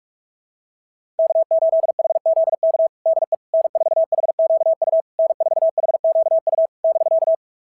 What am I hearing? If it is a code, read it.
Morse code, heard as a keyed tone, about 42 wpm, 660 Hz. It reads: K8HZK DE N4HQU N4HQU <BK>
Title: SOTA QSO Practice 45wpm